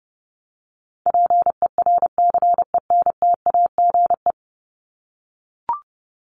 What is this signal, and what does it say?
Morse code, heard as a keyed tone, about 29 wpm, 700 Hz. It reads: PERCENTAGE
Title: Top 1001-1250 Words 30wpm